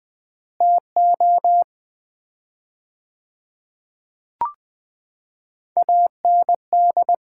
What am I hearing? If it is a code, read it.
Morse code, heard as a keyed tone, about 20 wpm, 700 Hz. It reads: TO AND
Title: 4x Word-Spacing - Sets of 4 Words - 20wpm